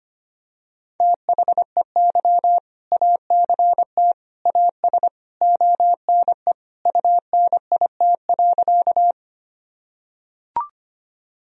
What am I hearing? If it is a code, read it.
Morse code, heard as a keyed tone, about 25 wpm, 700 Hz. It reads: THEY ACT AS ONE UNIT.
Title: Sentences from Top 1000 Words - Review Part 2 - 25wpm